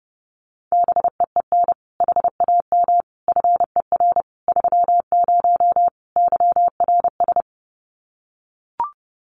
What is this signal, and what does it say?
Morse code, heard as a keyed tone, about 28 wpm, 700 Hz. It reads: BEEN HAM FER 30 YRS
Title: Common QSO Phrases Encore 30wpm